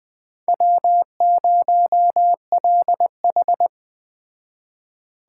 Morse code, heard as a keyed tone, about 21 wpm, 700 Hz. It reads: W0LH
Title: Speed Racing - USA 1x2 Call Signs 17wpm